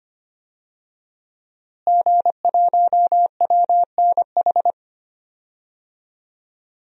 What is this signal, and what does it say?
Morse code, heard as a keyed tone, about 25 wpm, 700 Hz. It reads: G1WNH
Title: DXCC-100 Call Signs - 5 Characters - Repeated 3x - 25wpm